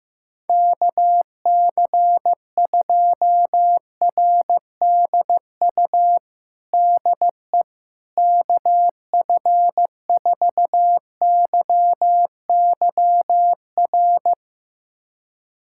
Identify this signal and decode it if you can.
Morse code, heard as a keyed tone, about 15 wpm, 700 Hz. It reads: KC2RDU DE KF4YYR
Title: USA Callsign DE Callsign 15wpm